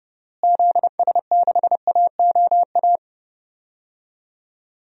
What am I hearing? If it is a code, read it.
Morse code, heard as a keyed tone, about 30 wpm, 700 Hz. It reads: ZS6AOA
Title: DXCC-100 Call Signs - 6 Characters - Repeated 3x - 30wpm